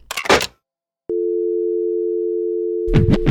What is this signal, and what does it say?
Dial tone (North America), 350 + 440 Hz.